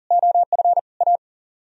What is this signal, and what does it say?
Morse code, heard as a keyed tone, about 51 wpm, 700 Hz. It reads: OF A